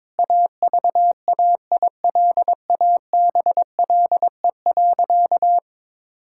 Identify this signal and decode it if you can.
Morse code, heard as a keyed tone, about 22 wpm, 700 Hz. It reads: AVAILABLE.
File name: Sentences with 7-Word Phrases - 22wpm